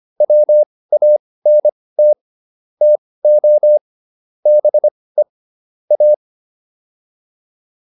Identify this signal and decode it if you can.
Morse code, heard as a keyed tone, about 25 wpm, spaced as for 18 wpm, 600 Hz. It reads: WANT TO BE A